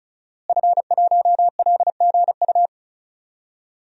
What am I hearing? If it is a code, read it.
Morse code, heard as a keyed tone, about 36 wpm, 700 Hz. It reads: F1LGU
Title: DXCC-100 Call Signs - 5 Characters - Repeated 2x - 35wpm